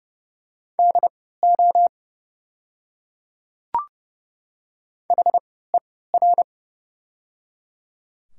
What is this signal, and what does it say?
Morse code, heard as a keyed tone, about 30 wpm, spaced as for 17 wpm, 700 Hz. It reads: DO HER
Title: 3x Character Spacing - Top 100 Words - 30wpm